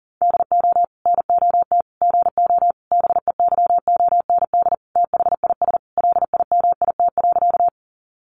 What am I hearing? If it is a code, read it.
Morse code, heard as a keyed tone, about 40 wpm, 700 Hz. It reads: DO NOT GO BEYOND THIS LIMIT.
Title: Sentences from Top 1500 Words 40wpm